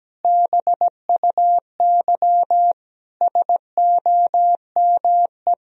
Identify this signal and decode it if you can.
Morse code, heard as a keyed tone, about 17 wpm, 700 Hz. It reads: BUY SOME
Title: Sentences from Top 1000 Words - Review Part 4 - 17wpm